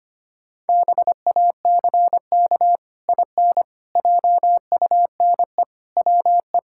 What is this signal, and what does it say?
Morse code, heard as a keyed tone, about 25 wpm, 700 Hz. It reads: BACK IN JUNE WE
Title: Pangram Challenge v2 - 25wpm